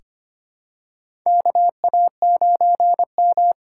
Morse code, heard as a keyed tone, about 25 wpm, 700 Hz. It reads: KA9M
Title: USA Call Signs - 4 Characters - 6x Character Spacing - 25wpm